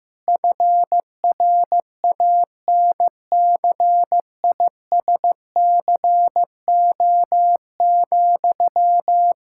Morse code, heard as a keyed tone, about 15 wpm, 700 Hz. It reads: FRANCISCO,
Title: Practice QSO - Exchange 1 - Beginning - 15wpm